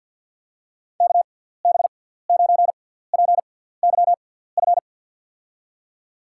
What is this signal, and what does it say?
Morse code, heard as a keyed tone, about 52 wpm, spaced as for 17 wpm, 700 Hz. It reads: KB9PYF
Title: USA Call Signs - 6 Characters - 6x Character Spacing - 50wpm